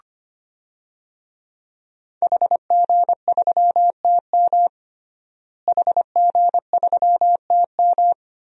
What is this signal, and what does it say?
Morse code, heard as a keyed tone, about 25 wpm, 700 Hz. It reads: HG3TM HG3TM
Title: DXCC-100 Call Signs - 5 Characters - Repeated 2x - 25wpm